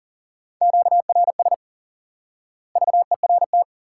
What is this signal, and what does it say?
Morse code, heard as a keyed tone, about 40 wpm, 700 Hz. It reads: QRS VERT
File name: IQE Course 3 - Lesson 89 - YAGI - Two Elements - 40wpm